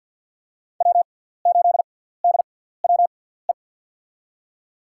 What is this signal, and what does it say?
Morse code, heard as a keyed tone, about 53 wpm, spaced as for 17 wpm, 690 Hz. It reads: W8DWE
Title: USA Call Signs - 5 Characters - 6x Character Spacing - 50wpm